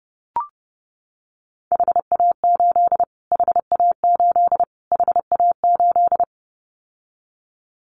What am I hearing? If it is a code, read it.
Morse code, heard as a keyed tone, about 30 wpm, 700 Hz. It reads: HA8 HA8 HA8
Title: Callsign Prefixes for Top 25 DXCC - Repeated 3x - 30wpm